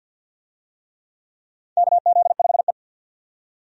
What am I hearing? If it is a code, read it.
Morse code, heard as a keyed tone, about 50 wpm, 700 Hz. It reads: KZ5E
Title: USA Call Signs - 4 Characters - 50wpm